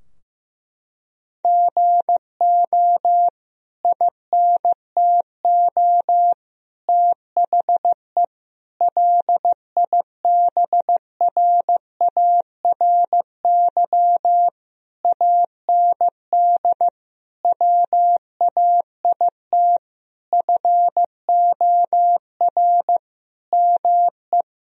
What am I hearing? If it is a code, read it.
Morse code, heard as a keyed tone, about 15 wpm, 700 Hz. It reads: GO INTO THE LIBRARY AND WAIT FOR ME
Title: Sentences from Top 1000 Words - Review Part 4 - 15wpm